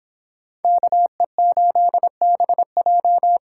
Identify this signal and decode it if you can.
Morse code, heard as a keyed tone, about 26 wpm, 700 Hz. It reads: KE8BJ